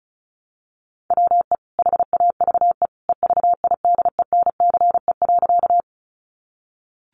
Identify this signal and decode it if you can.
Morse code, heard as a keyed tone, about 34 wpm, 700 Hz. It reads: WE HAVE EVIDENCE.
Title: Sentences from Top 1300 Words 35wpm